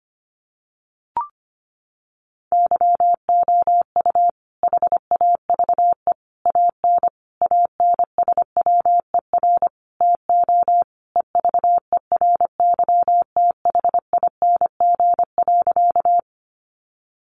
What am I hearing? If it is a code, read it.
Morse code, heard as a keyed tone, about 25 wpm, 700 Hz. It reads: YOU HAVE AN ANSWER TO EVERYTHING.
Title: Sentences from Top 700 Words 25wpm